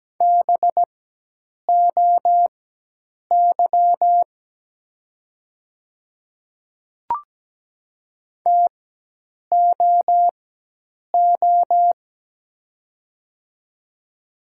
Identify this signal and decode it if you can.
Morse code, heard as a keyed tone, about 17 wpm, spaced as for 8 wpm, 690 Hz. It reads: BOY TOO
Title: 4x Character Spacing - Top 300 Words - 17wpm